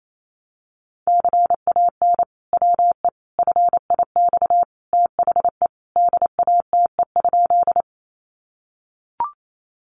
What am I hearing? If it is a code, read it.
Morse code, heard as a keyed tone, about 28 wpm, 700 Hz. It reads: CAN WE FIX THE DATE?
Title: Sentences from Top 1400 Words 28wpm